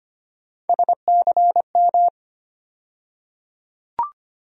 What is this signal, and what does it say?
Morse code, heard as a keyed tone, about 25 wpm, 700 Hz. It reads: SCM
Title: USA Call Sign Postfixes - 3 Letters 25wpm